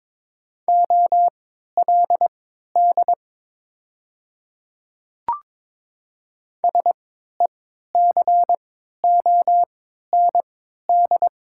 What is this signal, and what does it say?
Morse code, heard as a keyed tone, about 22 wpm, spaced as for 12 wpm, 700 Hz. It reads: OLD SECOND
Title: 3x Character Spacing - Top 300 Words - 22wpm